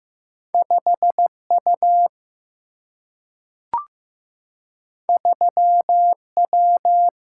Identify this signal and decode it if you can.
Morse code, heard as a keyed tone, about 15 wpm, 700 Hz. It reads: 5 U 3 W
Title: Single-Letter Single-Number 15wpm